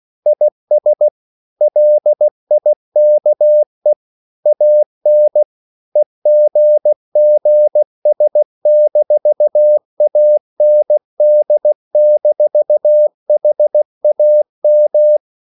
Morse code, heard as a keyed tone, about 16 wpm, 600 Hz. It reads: IS LIKE AN EGGS-AND-HAM
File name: CW / Practice / EnglishQuotations0116